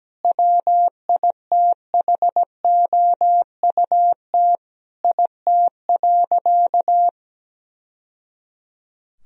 Morse code, heard as a keyed tone, about 17 wpm, 700 Hz. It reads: WITHOUT IT.
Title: Sentences from Top 1000 Words - Review Part 3 - 17wpm